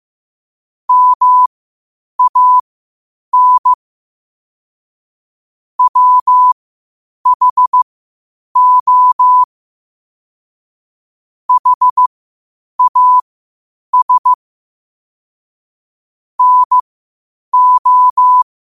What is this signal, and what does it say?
Morse code, heard as a keyed tone, about 15 wpm, spaced as for 8 wpm, 1000 Hz. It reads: MAN WHO HAS NO
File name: qotdfiles / 8wpm